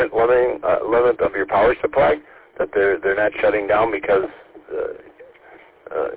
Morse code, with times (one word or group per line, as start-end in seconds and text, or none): none